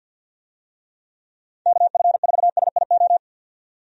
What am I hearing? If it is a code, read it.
Morse code, heard as a keyed tone, about 50 wpm, 700 Hz. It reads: KF4SIO
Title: USA Call Signs - 6 Characters - 50wpm